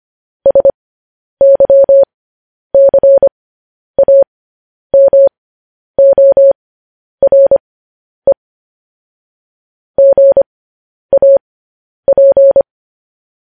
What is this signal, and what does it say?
Morse code, heard as a keyed tone, about 25 wpm, spaced as for 10 wpm, 550 Hz. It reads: SYCAMORE GAP
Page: News Headlines in Morse Code at 10 WPM